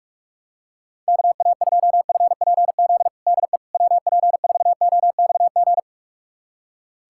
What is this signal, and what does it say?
Morse code, heard as a keyed tone, about 46 wpm, 690 Hz. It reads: KA2FPZ DE WP4OXG
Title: USA Callsign DE Callsign - 6 Characters - 45wpm